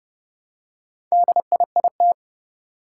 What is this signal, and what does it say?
Morse code, heard as a keyed tone, about 30 wpm, 700 Hz. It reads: DIIT
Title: Four Challenging Letter-Number Combinations 30wpm